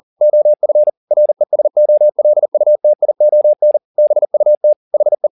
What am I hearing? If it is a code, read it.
Morse code, heard as a keyed tone, about 40 wpm, 600 Hz. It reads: OF RESOLUTION BUT HE
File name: the old man and the sea 2 40wpm